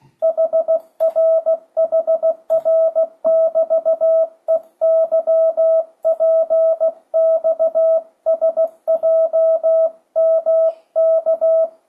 Morse code, heard as a keyed tone, about 15 wpm, 650 Hz. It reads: HRHR<BT>EYPXSJMK
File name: 2025cw / speedup / 16wpm